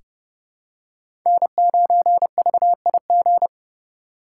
Morse code, heard as a keyed tone, about 30 wpm, 700 Hz. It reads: N9VIG
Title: USA Call Signs - 3x Character Spacing - 30wpm